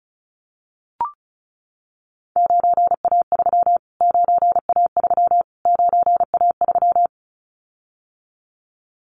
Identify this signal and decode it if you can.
Morse code, heard as a keyed tone, about 34 wpm, 700 Hz. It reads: 9A3 9A3 9A3
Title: Callsign Prefixes for Top 25 DXCC - Repeated 3x - 35wpm